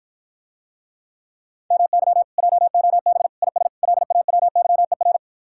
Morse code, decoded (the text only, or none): MY JOB IS LAWYER